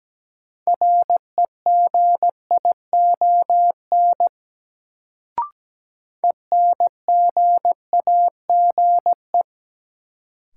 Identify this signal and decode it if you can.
Morse code, heard as a keyed tone, about 17 wpm, 700 Hz. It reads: REGION ENGAGE
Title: Sentences from Top 1500 Words - New Word Drill - 17wpm